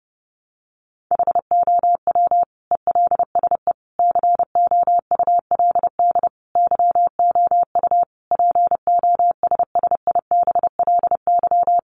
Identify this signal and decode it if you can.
Morse code, heard as a keyed tone, about 30 wpm, 700 Hz. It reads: HOW ELSE COULD YOU POSSIBLY
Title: Sentences from Top 1300 Words 30wpm